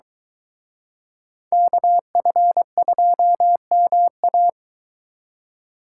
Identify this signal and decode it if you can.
Morse code, heard as a keyed tone, about 23 wpm, 700 Hz. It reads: KF2MA